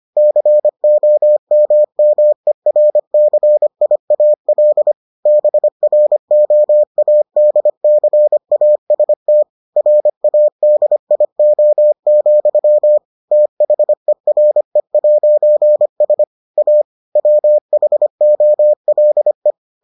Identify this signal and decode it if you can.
Morse code, heard as a keyed tone, about 25 wpm, 600 Hz. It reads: COMMERCIAL BROADCAST RADIO, THERE'S A WHOLE